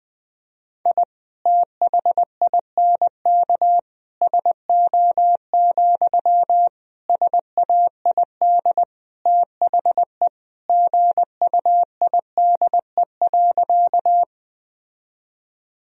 Morse code, heard as a keyed tone, about 20 wpm, 700 Hz. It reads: I THINK SO, SAID THE GUIDE.
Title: Sentences from Top 1000 Words - Review Part 3 - 20wpm